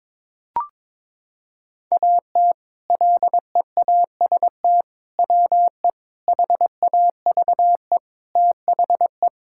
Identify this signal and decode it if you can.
Morse code, heard as a keyed tone, about 22 wpm, 700 Hz. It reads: AT LEAST WE HAVE THE
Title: Sentences from Top 1000 Words - Review Part 5 - 22wpm